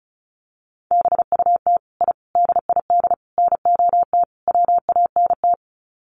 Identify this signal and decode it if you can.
Morse code, heard as a keyed tone, about 34 wpm, 700 Hz. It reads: BUT I DID NOT WANT
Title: Sets of 5 Words Encore 35wpm